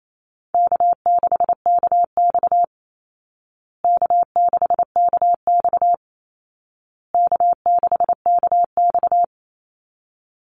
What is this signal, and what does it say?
Morse code, heard as a keyed tone, about 28 wpm, 700 Hz. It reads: K6KX K6KX K6KX
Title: Call Signs Repeated 2x - 28wpm